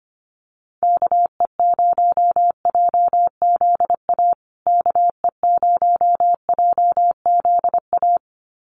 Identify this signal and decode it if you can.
Morse code, heard as a keyed tone, about 25 wpm, 700 Hz. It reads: KE0JZA KE0JZA
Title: USA Call Signs Repeated 2x 25wpm